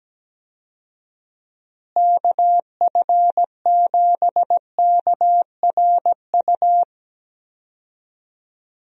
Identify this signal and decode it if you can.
Morse code, heard as a keyed tone, about 17 wpm, 700 Hz. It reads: KF7KRU
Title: USA Call Signs - 6 Characters - Repeated 3x - 17wpm